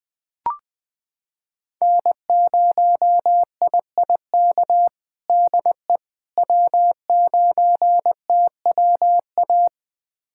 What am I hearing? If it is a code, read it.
Morse code, heard as a keyed tone, about 20 wpm, 700 Hz. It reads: N0IIK DE W9TWA
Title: USA Callsign DE Callsign - 5 Characters - 20wpm